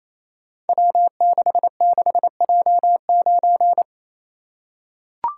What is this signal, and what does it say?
Morse code, heard as a keyed tone, about 28 wpm, 700 Hz. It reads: W66J9
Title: Code Groups - 5 Letter:Numbers - 28wpm